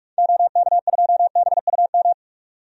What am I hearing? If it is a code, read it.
Morse code, heard as a keyed tone, about 44 wpm, 700 Hz. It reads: OK2BUM
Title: DXCC-100 Call Signs - 6 Characters - 45wpm